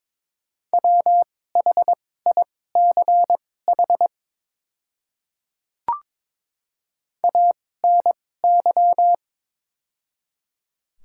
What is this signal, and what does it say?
Morse code, heard as a keyed tone, about 22 wpm, spaced as for 16 wpm, 700 Hz. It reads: WHICH ANY
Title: 2x Character Spacing - Top 300 Words - 22wpm